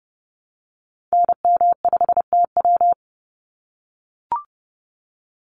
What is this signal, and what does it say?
Morse code, heard as a keyed tone, about 30 wpm, 700 Hz. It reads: NM5TW
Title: NA SOTA Activator Callsigns 30wpm